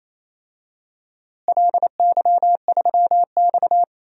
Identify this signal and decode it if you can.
Morse code, heard as a keyed tone, about 28 wpm, 700 Hz. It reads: LY3X
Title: DXCC-100 Call Signs - 4 Characters - Repeated 3x - 28wpm